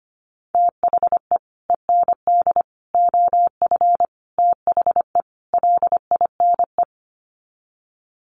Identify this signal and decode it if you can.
Morse code, heard as a keyed tone, about 25 wpm, 700 Hz. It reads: THE END OF THE LINE
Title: Sets of 5 Words Encore2 25wpm